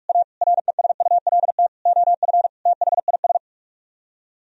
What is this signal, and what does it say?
Morse code, heard as a keyed tone, about 44 wpm, 700 Hz. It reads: A RESULT OF THIS